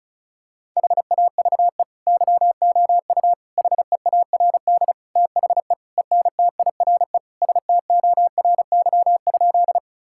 Morse code, heard as a keyed tone, about 36 wpm, 700 Hz. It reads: HAVE YOU HEARD THE ENTIRE STORY?
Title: Sentences from Top 1000 Words - Review Part 5 - 35wpm